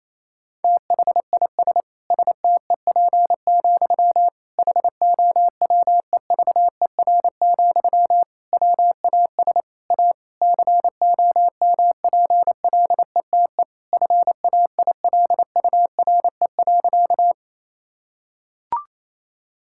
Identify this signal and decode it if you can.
Morse code, heard as a keyed tone, about 27 wpm, 700 Hz. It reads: THIS STEP, HOWEVER, WAS A COMPLETE FAILURE.
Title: Sentences from Top 1000 Words - Review Part 6 - 28wpm